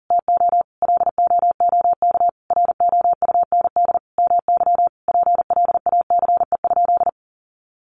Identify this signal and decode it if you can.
Morse code, heard as a keyed tone, about 40 wpm, 700 Hz. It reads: TO LOOK ROUND MY PLACE?
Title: Sentences from Top 200 Words 40wpm